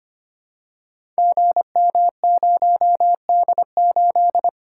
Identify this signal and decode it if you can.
Morse code, heard as a keyed tone, about 25 wpm, 690 Hz. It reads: GM0D8
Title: Code Groups - 5 Letter:Numbers - 25wpm